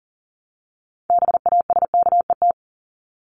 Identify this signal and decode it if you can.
Morse code, heard as a keyed tone, about 40 wpm, 700 Hz. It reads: BASKET
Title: Top 1751-2000 Words 40wpm